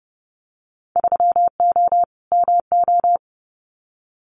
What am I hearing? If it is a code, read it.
Morse code, heard as a keyed tone, about 30 wpm, 700 Hz. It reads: 3O MO
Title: Winter Field Day Exchanges Encore 30wpm